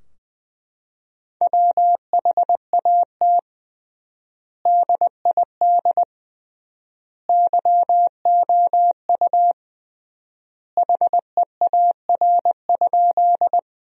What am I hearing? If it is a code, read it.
Morse code, heard as a keyed tone, about 20 wpm, 700 Hz. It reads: WHAT DID YOU HEAR?
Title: 3x Word-Spacing - Sentences from Top 300 Words - 20wpm